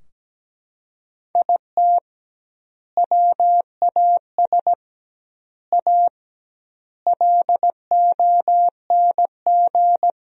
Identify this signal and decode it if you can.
Morse code, heard as a keyed tone, about 17 wpm, 700 Hz. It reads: IT WAS A LONG
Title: 2x Word-Spacing - Sentences from Top 500 Words - 17wpm